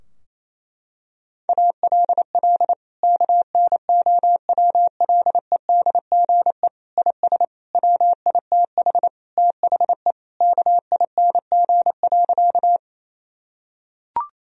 Morse code, heard as a keyed tone, about 28 wpm, 700 Hz. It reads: ALL KNOWLEDGE IS WITH THE KING.
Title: Sentences from Top 1000 Words - Review Part 3 - 28wpm